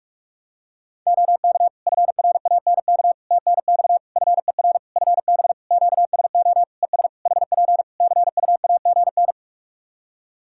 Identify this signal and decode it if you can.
Morse code, heard as a keyed tone, about 46 wpm, 700 Hz. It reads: OK FRANK TNX FER FB QSO ES HP CUAGN
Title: Practice QSO - Exchange 4 - Ending - 45wpm